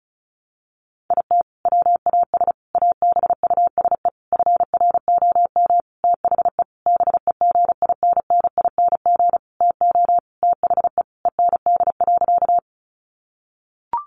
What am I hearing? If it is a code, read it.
Morse code, heard as a keyed tone, about 34 wpm, 700 Hz. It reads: IT WAS ABUSE FROM THE BEGINNING TO THE END.
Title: Sentences from Top 1300 Words 35wpm